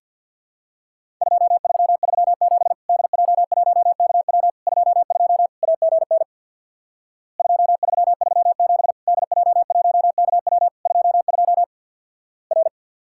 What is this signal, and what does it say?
Morse code, heard as a keyed tone, about 49 wpm, 700 Hz. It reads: 2337 DJ1KW 22 AGN 2337 DJ1KW 22 R
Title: Worked All Europe DX Contest - QTC Exchange with Repeats - 50wpm